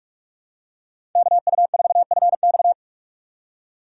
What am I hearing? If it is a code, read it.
Morse code, heard as a keyed tone, about 46 wpm, 700 Hz. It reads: KU4FX
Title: USA Call Signs - 4x Character Spacing - 45wpm